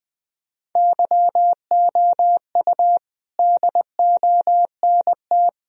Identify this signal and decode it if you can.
Morse code, heard as a keyed tone, about 20 wpm, 700 Hz. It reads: YOU DONT
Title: Sentences from Top 1000 Words - Review Part 1 - 20wpm